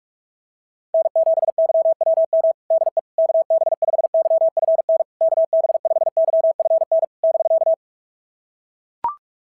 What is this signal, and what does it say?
Morse code, heard as a keyed tone, about 42 wpm, 660 Hz. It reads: N7YWM DE KB5YFN KB5YFN <BK>
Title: SOTA QSO Practice 45wpm